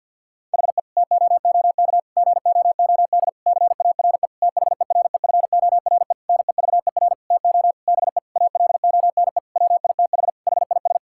Morse code, heard as a keyed tone, about 50 wpm, 690 Hz. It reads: HE TOOK GOOD CARE THEREFORE NEVER TO BE ALONE WITH HIS
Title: Sentences from Top 1000 Words - Review Part 5 - 50wpm